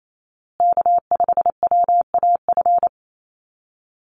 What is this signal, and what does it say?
Morse code, heard as a keyed tone, about 28 wpm, 700 Hz. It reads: K5WAF
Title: Call Signs Repeated 2x - 28wpm